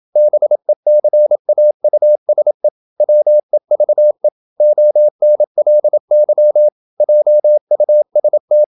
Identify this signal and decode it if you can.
Morse code, heard as a keyed tone, about 28 wpm, 600 Hz. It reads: BECAUSE WEVE ONLY JUST
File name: Never 0027wpm